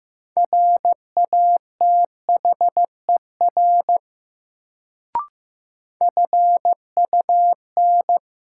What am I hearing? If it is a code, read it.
Morse code, heard as a keyed tone, about 15 wpm, 700 Hz. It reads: RATHER FUN
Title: Sentences from Top 700 Words - New Word Drill - 15wpm